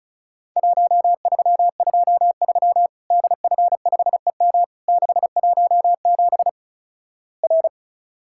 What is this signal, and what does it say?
Morse code, heard as a keyed tone, about 36 wpm, 700 Hz. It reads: 1323 DF5EM 617 R
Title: Worked All Europe DX Contest - QTC Exchange - 35wpm